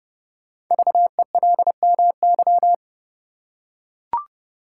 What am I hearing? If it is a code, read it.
Morse code, heard as a keyed tone, about 29 wpm, 690 Hz. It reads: VELMY
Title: Code Groups - 5 Letters - 30wpm